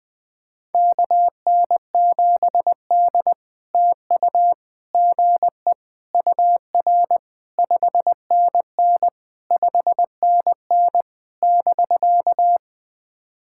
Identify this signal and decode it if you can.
Morse code, heard as a keyed tone, about 19 wpm, 700 Hz. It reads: KN7D TU GE UR 5NN 5NN <BK>
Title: POTA QSO Practice - 20wpm